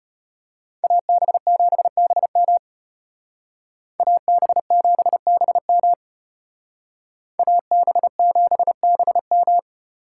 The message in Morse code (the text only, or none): AB7BM AB7BM AB7BM